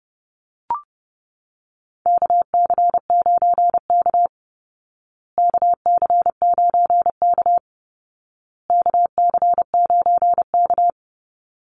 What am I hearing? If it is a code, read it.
Morse code, heard as a keyed tone, about 30 wpm, 700 Hz. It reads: KC9K KC9K KC9K
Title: Call Signs Repeated 2x - 30wpm